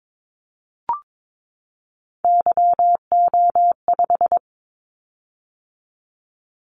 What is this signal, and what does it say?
Morse code, heard as a keyed tone, about 22 wpm, 700 Hz. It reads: YO5
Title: Callsign Prefixes for Top 25 DXCC - 22wpm